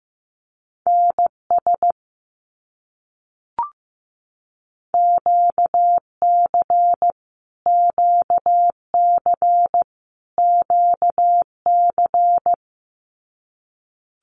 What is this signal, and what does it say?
Morse code, heard as a keyed tone, about 15 wpm, 700 Hz. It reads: NS QC QC QC
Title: Canadian Provinces and Territories - Repeated 3x - 15wpm